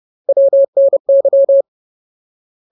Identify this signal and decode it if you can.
Morse code, heard as a keyed tone, about 30 wpm, 550 Hz. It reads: WNY
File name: Ultimate Word List / 30WPM Ultimate